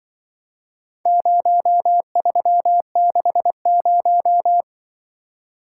Morse code, heard as a keyed tone, about 22 wpm, 700 Hz. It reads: 0360
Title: Speed-Racing - Four Letter-Number Combinations - 20wpm